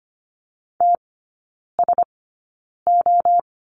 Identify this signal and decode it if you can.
Morse code, heard as a keyed tone, about 25 wpm, spaced as for 9 wpm, 700 Hz. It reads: TSO
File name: course / 25:10wpm / 093 Lesson 16e - Letter c - Three Characters 25wpm